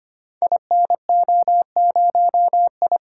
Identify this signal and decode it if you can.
Morse code, heard as a keyed tone, about 25 wpm, 700 Hz. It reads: INO0I
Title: Code Groups - 5 Letter:Numbers - 25wpm